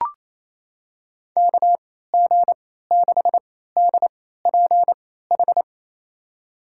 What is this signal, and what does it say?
Morse code, heard as a keyed tone, about 28 wpm, spaced as for 16 wpm, 700 Hz. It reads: KG6DPH
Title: USA Call Signs - 6 Characters - 3x Character Spacing - 28wpm